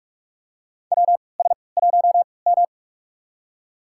Morse code, heard as a keyed tone, about 46 wpm, spaced as for 26 wpm, 700 Hz. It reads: WS1M
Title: USA Call Signs - 4 Characters - 3x Character Spacing - 45wpm